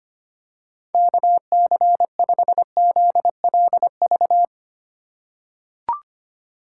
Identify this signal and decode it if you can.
Morse code, heard as a keyed tone, about 25 wpm, 700 Hz. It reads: KC5ZLV
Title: USA Call Signs - 6 Characters - 2x Character Spacing - 25wpm